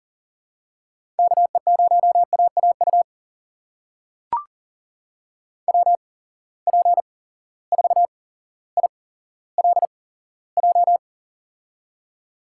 Morse code, heard as a keyed tone, about 38 wpm, spaced as for 17 wpm, 700 Hz. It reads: KE0AAU WP4ILJ
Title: USA Call Signs - 6 Characters - 8x Character Spacing - 40wpm